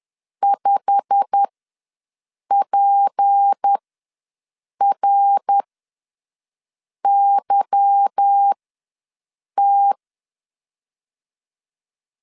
Morse code, heard as a keyed tone, about 10 wpm, spaced as for 6 wpm, 800 Hz. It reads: 5PRYT
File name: LEZIONI / 8 YJ45 / 8 YJ45 ws 10wpm 3843